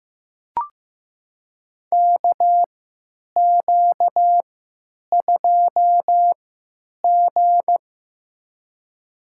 Morse code, heard as a keyed tone, about 15 wpm, spaced as for 9 wpm, 700 Hz. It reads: KQ2G
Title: USA Call Signs - 4 Characters - 3x Character Spacing - 15wpm